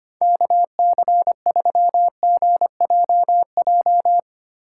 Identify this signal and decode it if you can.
Morse code, heard as a keyed tone, about 25 wpm, 700 Hz. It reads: KC3GJJ
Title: USA Call Signs - 6 Characters - 2x Character Spacing - 25wpm